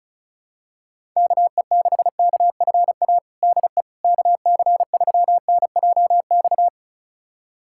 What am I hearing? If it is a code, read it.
Morse code, heard as a keyed tone, about 36 wpm, 700 Hz. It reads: KE6KFA DE KC3NJX
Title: USA Callsign DE Callsign - 6 Characters - 35wpm